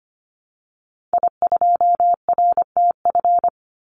Morse code, heard as a keyed tone, about 25 wpm, 700 Hz. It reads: I2RTF
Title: Call Signs - 25wpm